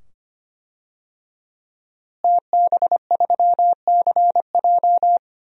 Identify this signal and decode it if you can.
Morse code, heard as a keyed tone, about 25 wpm, 700 Hz. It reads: TB3CJ
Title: DXCC-100 Call Signs - 5 Characters - Repeated 2x - 25wpm